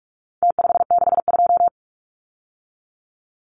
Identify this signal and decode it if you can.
Morse code, heard as a keyed tone, about 44 wpm, 700 Hz. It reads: T562
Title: Numbers - Contest Style - 4-Digit 45wpm